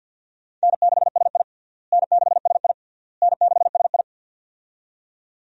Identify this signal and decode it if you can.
Morse code, heard as a keyed tone, about 52 wpm, 700 Hz. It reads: N6SI N6SI N6SI
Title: USA Call Signs - 4 Characters - Repeated 3x - 50wpm